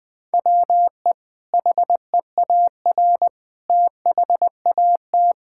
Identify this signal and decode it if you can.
Morse code, heard as a keyed tone, about 21 wpm, 700 Hz. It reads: WE HEAR THAT
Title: Sentences from Top 1000 Words - Review Part 5 - 20wpm